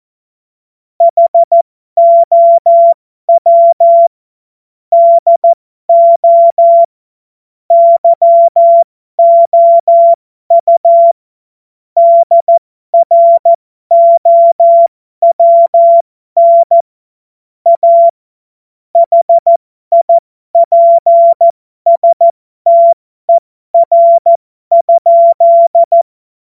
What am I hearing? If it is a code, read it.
Morse code, heard as a keyed tone, about 14 wpm, spaced as for 12 wpm, 680 Hz. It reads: HOW DO YOU DROWN A HIPSTER?
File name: Morsekurs / Witze